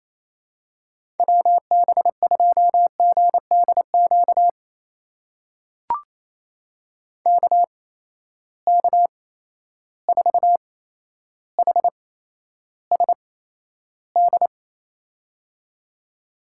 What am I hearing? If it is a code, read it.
Morse code, heard as a keyed tone, about 28 wpm, spaced as for 12 wpm, 700 Hz. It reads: WB2GDQ KK4HSD
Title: USA Call Signs - 8x Character Spacing - 28wpm